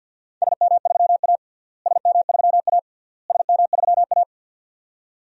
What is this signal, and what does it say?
Morse code, heard as a keyed tone, about 48 wpm, 700 Hz. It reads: SM3A SM3A SM3A
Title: DXCC-100 Call Signs - 4 Characters - Repeated 3x - 50wpm